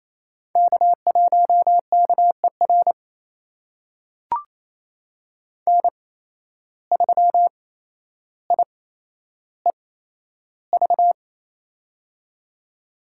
Morse code, heard as a keyed tone, about 28 wpm, spaced as for 12 wpm, 700 Hz. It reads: K1KER N3IEV
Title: USA Call Signs - 5 Characters - 8x Character Spacing - 28wpm